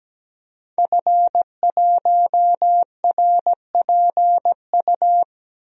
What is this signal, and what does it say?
Morse code, heard as a keyed tone, about 17 wpm, 700 Hz. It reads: F1RPU
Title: DXCC-100 Call Signs - 5 Characters - Repeated 2x - 17wpm